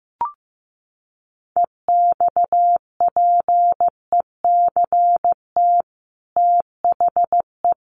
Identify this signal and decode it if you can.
Morse code, heard as a keyed tone, about 15 wpm, 700 Hz. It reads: EXPECT THE
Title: Sentences from Top 1100 Words 15wpm